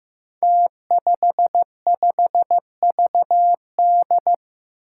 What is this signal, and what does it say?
Morse code, heard as a keyed tone, about 15 wpm, 700 Hz. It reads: T55VD
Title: Five Challenging Letter-Number Combinations 15wpm